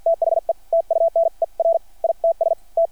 Morse code, heard as a keyed tone, about 52 wpm, spaced as for 43 wpm, 660 Hz. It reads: THE TUNER ITS T
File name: archived / CRSnet